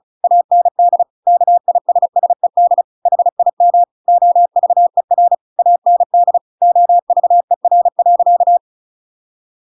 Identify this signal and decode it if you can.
Morse code, heard as a keyed tone, about 34 wpm, 700 Hz. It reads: AND KISSED HIM OVER AND OVER.
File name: Pinocchio / Pinocchio ch8 35 35 ik1hge